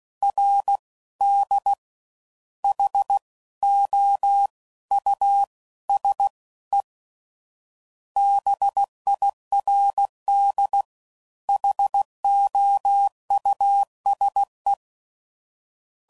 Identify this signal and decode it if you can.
Morse code, heard as a keyed tone, about 16 wpm, 800 Hz. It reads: R D H O U S E BIRD HOUSE
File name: Suffixes / General / Houses